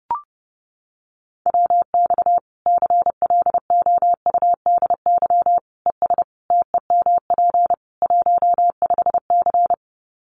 Morse code, heard as a keyed tone, about 30 wpm, 700 Hz. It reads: WX CLOUDY ES TEMP 15C
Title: Common QSO Phrases 30wpm